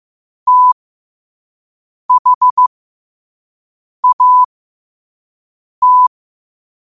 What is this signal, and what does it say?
Morse code, heard as a keyed tone, about 15 wpm, spaced as for 5 wpm, 1000 Hz. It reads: THAT
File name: qotdfiles / 5wpm